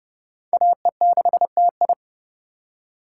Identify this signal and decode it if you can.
Morse code, heard as a keyed tone, about 35 wpm, 700 Hz. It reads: AE6TI
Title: Speed Racing - USA 2x2 Call Signs 30wpm